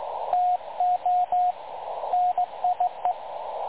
Morse code, heard as a keyed tone, about 19 wpm, 720 Hz, with radio noise. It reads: TO NIE
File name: dlazaawansowanych / nagraniazpasma / websdr recording 3565.0kHz